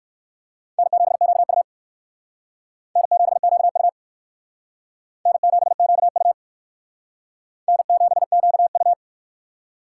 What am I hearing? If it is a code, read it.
Morse code, heard as a keyed tone, about 56 wpm, 700 Hz. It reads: N7QU N7QU N7QU N7QU
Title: Speed Racing - USA 1x2 Call Signs 45wpm